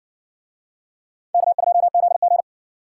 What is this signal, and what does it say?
Morse code, heard as a keyed tone, about 60 wpm, 700 Hz. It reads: K2ZG